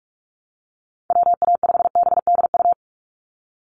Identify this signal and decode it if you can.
Morse code, heard as a keyed tone, about 44 wpm, 700 Hz. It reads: WA5BDU
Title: Call Signs Repeated 2x - 45wpm